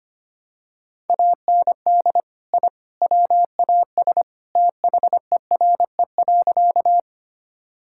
Morse code, heard as a keyed tone, about 25 wpm, 700 Hz. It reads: AND I WAS THERE.
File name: practice / 25wpm / Sentences with 3-Word Phrases - 25wpm